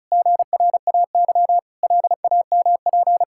Morse code, heard as a keyed tone, about 34 wpm, 700 Hz. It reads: GRAY LAMP
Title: Pangram Challenge v2 - 35wpm